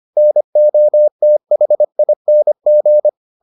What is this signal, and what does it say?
Morse code, heard as a keyed tone, about 25 wpm, 600 Hz. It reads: NOTHING